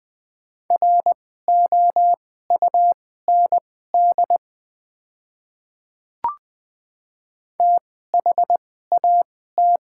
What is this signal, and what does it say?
Morse code, heard as a keyed tone, about 20 wpm, spaced as for 15 wpm, 700 Hz. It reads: ROUND THAT
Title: 2x Character Spacing - Top 200 Words - 20wpm